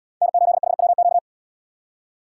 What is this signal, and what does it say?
Morse code, heard as a keyed tone, about 70 wpm, 700 Hz. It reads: N8HWQ